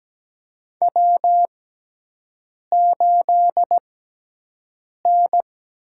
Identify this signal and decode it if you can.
Morse code, heard as a keyed tone, about 17 wpm, spaced as for 6 wpm, 700 Hz. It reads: W8N
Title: USA Call Signs - 4 Characters - 6x Character Spacing - 17wpm